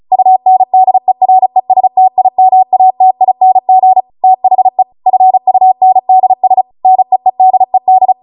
Morse code, heard as a keyed tone, about 36 wpm, 750 Hz. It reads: UNDERESTIMATING THE FUNDS NEEDED
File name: Morse / 35WPM